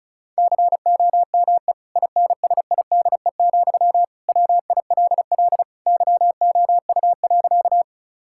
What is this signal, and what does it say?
Morse code, heard as a keyed tone, about 36 wpm, 700 Hz. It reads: COME INSIDE, WILL YOU.
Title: Sentences from Top 1000 Words - Review Part 6 - 35wpm